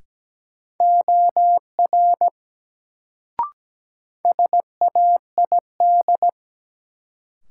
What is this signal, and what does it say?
Morse code, heard as a keyed tone, about 17 wpm, 700 Hz. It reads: OR SAID
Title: Sets of 2 Words - New Word Drill - 17wpm